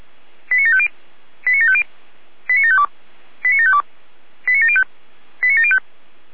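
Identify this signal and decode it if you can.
Five-tone selective call (ZVEI), heard as quick runs of five tones, tones of about 70 ms. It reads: 89750 89750 89752 89752 89705 89705
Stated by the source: Five-tone ZVEI selective call 89701, received by SDR.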